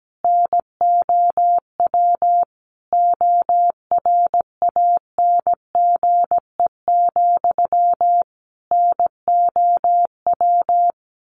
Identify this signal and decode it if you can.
Morse code, heard as a keyed tone, about 17 wpm, 700 Hz. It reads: NOW ORANGE, NOW
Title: A Sound Of Thunder 17wpm